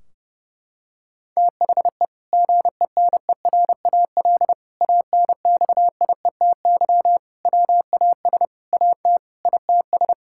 Morse code, heard as a keyed tone, about 30 wpm, 700 Hz. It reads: THE GENERAL ANXIETY WAS AT ITS
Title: Sentences from Top 1600 Words 30wpm